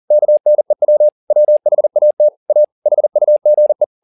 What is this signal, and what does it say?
Morse code, heard as a keyed tone, about 40 wpm, 600 Hz. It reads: KNEW WHAT A HUGE